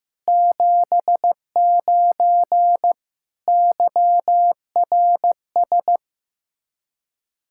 Morse code, heard as a keyed tone, about 15 wpm, 700 Hz. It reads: 79 YRS
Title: Practice QSO - Exchange 3 - More Info - 15wpm